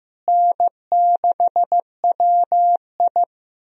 Morse code, heard as a keyed tone, about 15 wpm, 700 Hz. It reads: N6WI
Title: USA Call Signs - 4 Characters - Repeated 3x - 15wpm